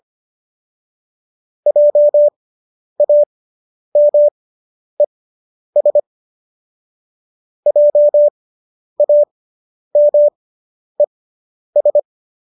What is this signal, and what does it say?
Morse code, heard as a keyed tone, about 25 wpm, spaced as for 10 wpm, 590 Hz. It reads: JAMES JAMES